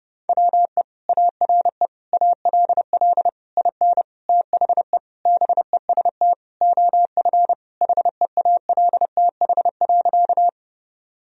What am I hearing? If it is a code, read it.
Morse code, heard as a keyed tone, about 30 wpm, 700 Hz. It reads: WE ARE ALL IN THE BEST OF HEALTH.
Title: Sentences from Top 1000 Words - Review Part 3 - 30wpm